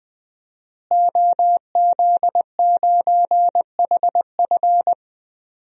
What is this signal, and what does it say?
Morse code, heard as a keyed tone, about 20 wpm, 700 Hz. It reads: OZ9HF
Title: DXCC-100 Call Signs - 5 Characters - Repeated 3x - 20wpm